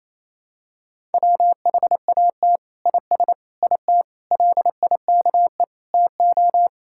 Morse code, heard as a keyed tone, about 28 wpm, 700 Hz. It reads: WHAT IS IT LIKE TO
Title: Sentences from Top 1000 Words - Review Part 6 - 28wpm